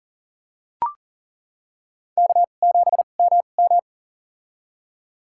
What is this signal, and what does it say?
Morse code, heard as a keyed tone, about 42 wpm, spaced as for 29 wpm, 690 Hz. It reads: K7MM
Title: USA Call Signs - 4 Characters - 2x Character Spacing - 40wpm